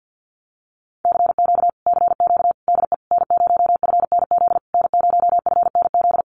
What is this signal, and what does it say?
Morse code, heard as a keyed tone, about 48 wpm, 700 Hz. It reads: CQ CQ DE N0FNZ N0FNZ
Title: Common QSO Phrases Encore 50wpm